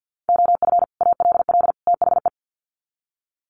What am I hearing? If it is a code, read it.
Morse code, heard as a keyed tone, about 50 wpm, 700 Hz. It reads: OF ALL THE
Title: Sets of 3 Words 50wpm